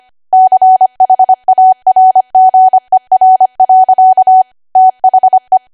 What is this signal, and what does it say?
Morse code, heard as a keyed tone, about 25 wpm, 750 Hz. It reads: CHARGER. THE
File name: Archive / 25 WPM / 25WPM